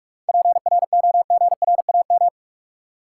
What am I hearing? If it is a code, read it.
Morse code, heard as a keyed tone, about 47 wpm, 700 Hz. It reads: PROGRAM
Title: Sentences from Top 1000 Words - Review Part 4 - New Word Drill - 45wpm